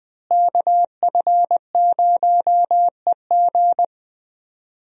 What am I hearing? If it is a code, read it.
Morse code, heard as a keyed tone, about 20 wpm, 700 Hz. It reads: KF0EG
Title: USA Call Signs - 5 Characters - Repeated 2x - 20wpm